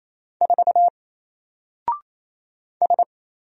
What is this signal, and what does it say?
Morse code, heard as a keyed tone, about 27 wpm, 700 Hz. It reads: 4 S
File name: Single-Letter-Number-28wpm